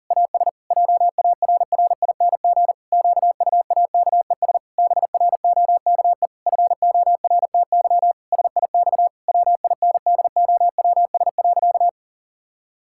Morse code, heard as a keyed tone, about 40 wpm, 700 Hz. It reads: AS JARRING QUAKES BROKE FORTY SIX WINDOWS.
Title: Pangram Challenge v2 - 40wpm